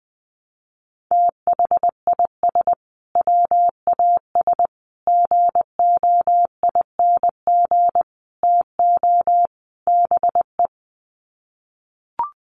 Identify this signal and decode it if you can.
Morse code, heard as a keyed tone, about 20 wpm, 700 Hz. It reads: THIS WAS GOING TO BE
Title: Sets of 5 Words Encore2 20wpm